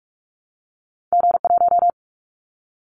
Morse code, heard as a keyed tone, about 44 wpm, 700 Hz. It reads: G1
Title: Callsign Prefixes for Top 25 DXCC - 45wpm